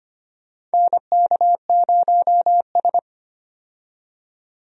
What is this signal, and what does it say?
Morse code, heard as a keyed tone, about 25 wpm, 700 Hz. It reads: NK0S
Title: USA Call Signs - 4 Characters - 25wpm